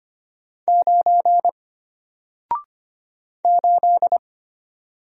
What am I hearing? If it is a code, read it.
Morse code, heard as a keyed tone, about 25 wpm, 700 Hz. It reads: 9 8
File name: Single-Letter-Number-25wpm